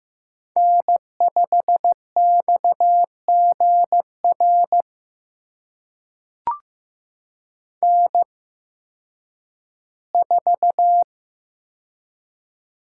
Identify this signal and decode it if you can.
Morse code, heard as a keyed tone, about 15 wpm, spaced as for 10 wpm, 700 Hz. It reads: N5XGR N4
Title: USA Call Signs - 5 Characters - 8x Character Spacing - 15wpm